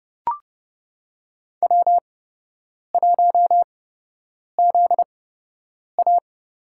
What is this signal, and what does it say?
Morse code, heard as a keyed tone, about 29 wpm, spaced as for 8 wpm, 700 Hz. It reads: W1ZA
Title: USA Call Signs - 8x Character Spacing - 30wpm